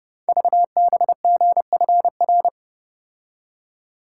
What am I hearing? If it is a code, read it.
Morse code, heard as a keyed tone, about 29 wpm, 690 Hz. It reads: VBGFR
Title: Code Groups - 5 Letters - 30wpm